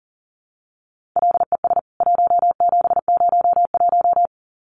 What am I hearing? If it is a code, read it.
Morse code, heard as a keyed tone, about 40 wpm, 700 Hz. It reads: LES 1701
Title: CWT Exchange 40wpm